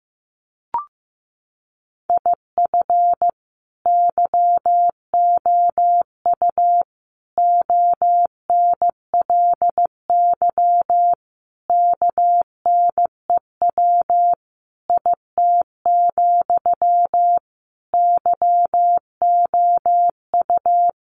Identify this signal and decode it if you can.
Morse code, heard as a keyed tone, about 15 wpm, 700 Hz. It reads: IF YOU ONLY KNEW IT, YOU
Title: Sentences from Top 1500 Words 15wpm